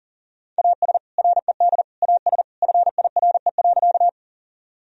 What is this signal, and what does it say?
Morse code, heard as a keyed tone, about 37 wpm, 700 Hz. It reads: AS RED AS FIRE.
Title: Sentences from Top 1000 Words - Review Part 2 - 40wpm